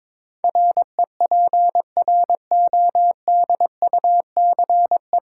Morse code, heard as a keyed tone, about 22 wpm, 700 Hz. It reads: REPRODUCE
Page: Morse Code Ninja practice 22wpm